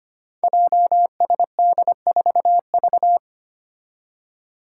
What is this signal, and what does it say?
Morse code, heard as a keyed tone, about 25 wpm, 700 Hz. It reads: JSD4V
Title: Five Challenging Letter-Number Combinations 25wpm